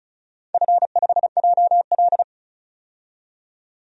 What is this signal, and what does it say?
Morse code, heard as a keyed tone, about 36 wpm, 700 Hz. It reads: F5JL
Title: DXCC-100 Call Signs - 4 Characters - 35wpm